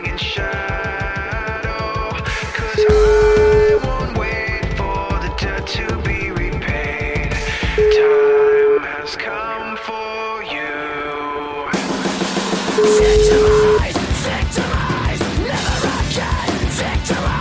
Ringback tone (Europe, 425 Hz), with noise.